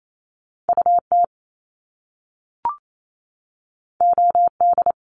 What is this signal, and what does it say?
Morse code, heard as a keyed tone, about 28 wpm, 700 Hz. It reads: UT OD
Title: Top Two Letter Combinations in Words 28wpm